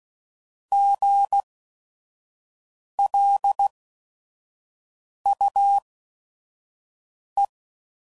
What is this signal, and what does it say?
Morse code, heard as a keyed tone, about 16 wpm, spaced as for 5 wpm, 800 Hz. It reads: GLUE
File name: Prefixes / By First Letter / Gs